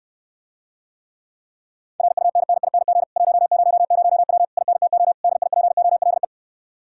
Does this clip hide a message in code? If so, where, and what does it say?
Yes: Morse code from 2.0 to 6.2 s, heard as a keyed tone, about 69 wpm, 700 Hz. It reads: RUNNING 100W INTO DIPOLE